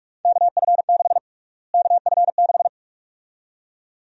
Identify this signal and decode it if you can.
Morse code, heard as a keyed tone, about 44 wpm, 700 Hz. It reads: KF6 KF6
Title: USA Call Sign Prefixes - Repeated 2x - 45wpm